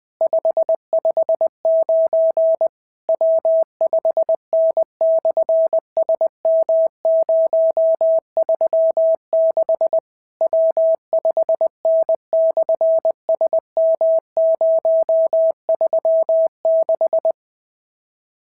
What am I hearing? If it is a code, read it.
Morse code, heard as a keyed tone, about 20 wpm, 650 Hz. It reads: 559 W5N/SM036 W5N/SM036
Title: SOTA QSO Practice 20wpm